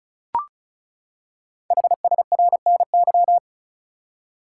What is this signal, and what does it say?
Morse code, heard as a keyed tone, about 36 wpm, 690 Hz. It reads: HSRNY
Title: Code Groups - 5 Letters - 35wpm